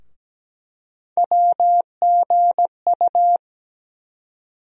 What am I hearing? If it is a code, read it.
Morse code, heard as a keyed tone, about 17 wpm, 700 Hz. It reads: WGU
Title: USA Call Sign Postfixes - 3 Letters 17wpm